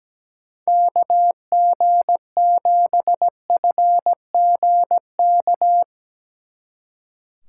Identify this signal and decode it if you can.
Morse code, heard as a keyed tone, about 20 wpm, 700 Hz. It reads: KG7FGK
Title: Speed Racing - USA 2x3 Call Signs 17wpm